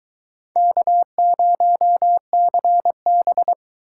Morse code, heard as a keyed tone, about 23 wpm, 700 Hz. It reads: K0CB